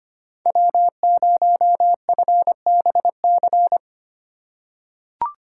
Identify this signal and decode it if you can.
Morse code, heard as a keyed tone, about 25 wpm, 700 Hz. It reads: W0FBC
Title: USA Call Signs - 5 Characters - 4x Character Spacing - 25wpm